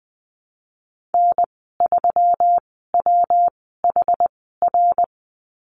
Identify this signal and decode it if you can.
Morse code, heard as a keyed tone, about 20 wpm, spaced as for 15 wpm, 700 Hz. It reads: N3WHR
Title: 140 Lesson 23g - Letter f - Call Signs 20wpm